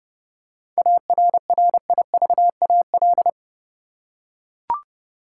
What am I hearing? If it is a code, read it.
Morse code, heard as a keyed tone, about 31 wpm, 700 Hz. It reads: ARRIVAL
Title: Seven Letter Words - 30wpm